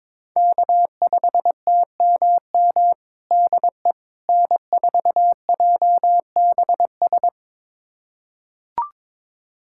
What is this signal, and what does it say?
Morse code, heard as a keyed tone, about 22 wpm, 700 Hz. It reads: K5TMM DE N4JBS
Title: USA Callsign DE Callsign - 5 Characters - 22wpm